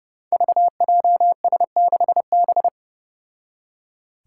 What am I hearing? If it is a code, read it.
Morse code, heard as a keyed tone, about 30 wpm, 700 Hz. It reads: VJS6B
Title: Five Challenging Letter-Number Combinations 30wpm